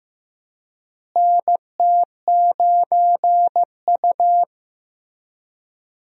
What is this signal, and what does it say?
Morse code, heard as a keyed tone, about 15 wpm, 700 Hz. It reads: NT9U
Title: USA Call Signs - 4 Characters - Repeated 3x - 15wpm